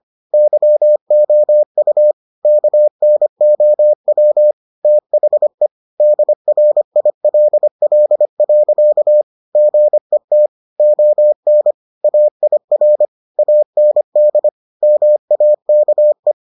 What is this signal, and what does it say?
Morse code, heard as a keyed tone, about 25 wpm, 600 Hz. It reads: YOU KNOW THE DRILL. GET ON AIR AND MAKE